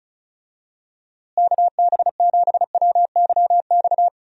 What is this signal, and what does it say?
Morse code, heard as a keyed tone, about 36 wpm, 700 Hz. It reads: KB7WYX
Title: USA Call Signs - 6 Characters - Repeated 2x - 35wpm